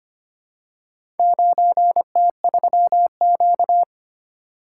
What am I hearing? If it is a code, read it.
Morse code, heard as a keyed tone, about 25 wpm, 700 Hz. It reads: 9T3Q
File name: practice / 25wpm / Four Letter-Number Combinations - 25wpm